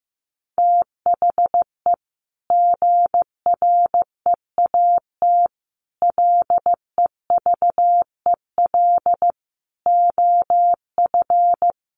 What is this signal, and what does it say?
Morse code, heard as a keyed tone, about 15 wpm, 700 Hz. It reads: THE GREAT LEVEL OF
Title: A Sound Of Thunder 15wpm